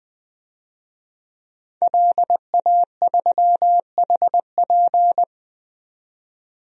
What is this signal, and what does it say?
Morse code, heard as a keyed tone, about 20 wpm, 700 Hz. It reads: LA3HP
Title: DXCC-100 Call Signs - 5 Characters - 20wpm